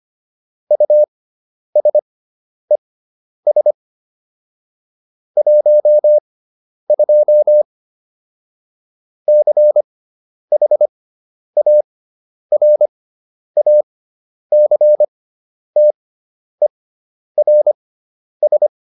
Morse code, heard as a keyed tone, about 25 wpm, spaced as for 10 wpm, 600 Hz. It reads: USES 12 CHARACTERS